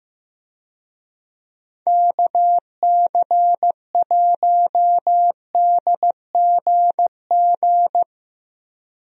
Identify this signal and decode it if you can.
Morse code, heard as a keyed tone, about 15 wpm, 700 Hz. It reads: KC1DGG
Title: USA Call Signs - 6 Characters - Repeated 2x - 15wpm